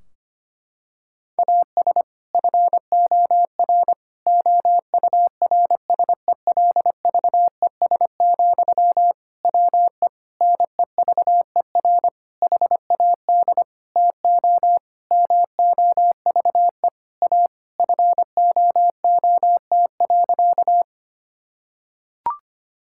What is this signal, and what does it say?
Morse code, heard as a keyed tone, about 25 wpm, 700 Hz. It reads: AS FOR OURSELVES, WE NEVER HAD TO MOVE A FOOT.
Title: Sentences from Top 1600 Words 25wpm